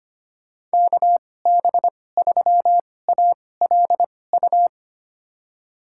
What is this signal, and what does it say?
Morse code, heard as a keyed tone, about 25 wpm, spaced as for 18 wpm, 700 Hz. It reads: KB3ALU
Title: USA Call Signs - 6 Characters - 2x Character Spacing - 25wpm